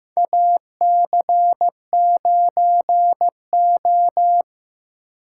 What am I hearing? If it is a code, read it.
Morse code, heard as a keyed tone, about 15 wpm, 700 Hz. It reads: AC9O
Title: USA Call Signs - 4 Characters - 4x Character Spacing - 15wpm